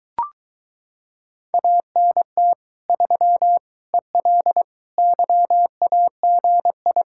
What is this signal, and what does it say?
Morse code, heard as a keyed tone, about 23 wpm, 700 Hz. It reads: ANT 3 EL YAGI